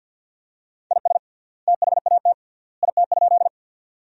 Morse code, heard as a keyed tone, about 51 wpm, 700 Hz. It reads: IS THAT IT?
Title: 3x Word-Spacing - Sentences from Top 100 Words - 50wpm